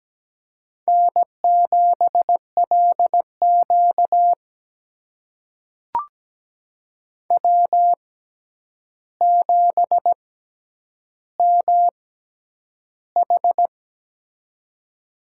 Morse code, heard as a keyed tone, about 17 wpm, spaced as for 9 wpm, 700 Hz. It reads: N7LQ W7MH
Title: USA Call Signs - 4 Characters - 6x Character Spacing - 17wpm